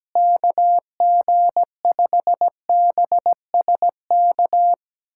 Morse code, heard as a keyed tone, about 17 wpm, 700 Hz. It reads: KG5BSK
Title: USA Callsign DE Callsign 17wpm